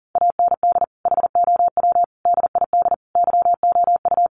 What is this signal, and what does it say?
Morse code, heard as a keyed tone, about 40 wpm, 700 Hz. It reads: AND HOW DID YOU
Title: Sentences from Top 1200 Words 40wpm